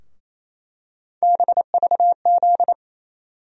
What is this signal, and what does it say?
Morse code, heard as a keyed tone, about 28 wpm, 700 Hz. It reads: BVZ
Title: USA Call Sign Postfixes - 3 Letters 28wpm